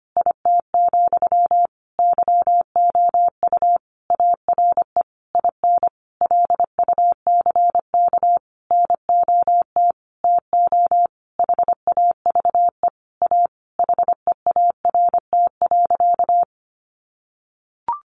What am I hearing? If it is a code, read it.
Morse code, heard as a keyed tone, about 25 wpm, 700 Hz. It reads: IT, YOU ARE IN LUCK NOT TO HAVE A HEART.
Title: Sentences from Top 1500 Words 25wpm